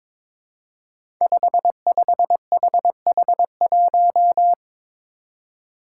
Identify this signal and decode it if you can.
Morse code, heard as a keyed tone, about 22 wpm, 700 Hz. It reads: 55HH1
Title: Five Challenging Letter-Number Combinations 22wpm